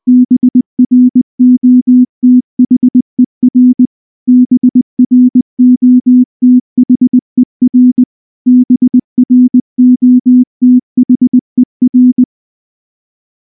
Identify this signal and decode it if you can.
Morse code, heard as a keyed tone, about 19 wpm, 250 Hz. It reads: BROTHER BROTHER BROTHER
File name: morse / morse-training-20WPM-4